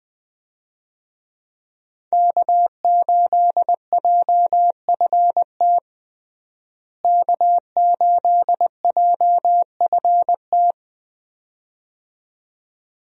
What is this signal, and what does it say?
Morse code, heard as a keyed tone, about 21 wpm, 700 Hz. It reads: K8JFT K8JFT
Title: USA Call Signs - 5 Characters - Repeated 2x - 20wpm